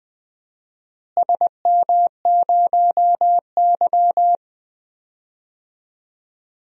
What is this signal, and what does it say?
Morse code, heard as a keyed tone, about 20 wpm, 700 Hz. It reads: SM0Y
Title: DXCC-100 Call Signs - 4 Characters - Repeated 2x - 20wpm